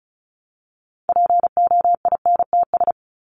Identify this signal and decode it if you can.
Morse code, heard as a keyed tone, about 34 wpm, 700 Hz. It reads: POINTS
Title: ES:S Suffix - Top 100 Words with Base Words 35wpm